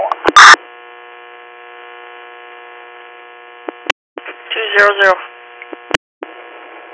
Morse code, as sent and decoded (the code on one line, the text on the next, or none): none